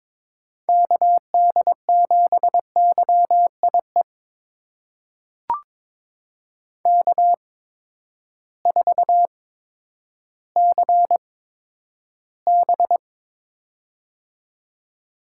Morse code, heard as a keyed tone, about 23 wpm, spaced as for 11 wpm, 700 Hz. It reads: KD7YIE K4CB
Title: USA Call Signs - 8x Character Spacing - 22wpm